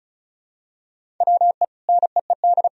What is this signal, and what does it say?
Morse code, heard as a keyed tone, about 36 wpm, 700 Hz. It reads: WE NEED
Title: Sentences from Top 1000 Words - Review Part 4 - 35wpm